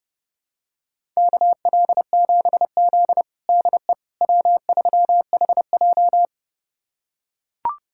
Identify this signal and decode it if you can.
Morse code, heard as a keyed tone, about 32 wpm, 700 Hz. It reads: KL7Z DE W3HJ
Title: USA Callsign DE Callsign - 4 Characters - 30wpm